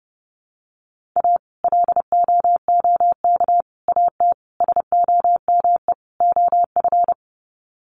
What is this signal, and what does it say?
Morse code, heard as a keyed tone, about 30 wpm, 700 Hz. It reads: A LOOK AT SOME OF
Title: Sets of 5 Words Encore2 30wpm